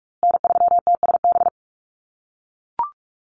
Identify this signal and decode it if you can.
Morse code, heard as a keyed tone, about 44 wpm, 700 Hz. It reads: N3TSB
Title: USA Call Signs Repeated 2x 45wpm